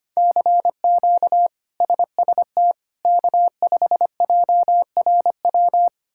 Morse code, heard as a keyed tone, about 25 wpm, 700 Hz. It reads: CQ SST K5JRW
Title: K1USN SST QSO Practice - 25wpm